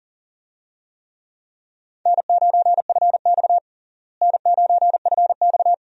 Morse code, heard as a keyed tone, about 40 wpm, 700 Hz. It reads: N9FX N9FX
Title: USA Call Signs - 4 Characters - Repeated 2x - 40wpm